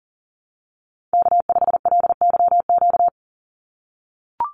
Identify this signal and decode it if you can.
Morse code, heard as a keyed tone, about 40 wpm, 700 Hz. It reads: K5LYQ
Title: USA Call Signs 40wpm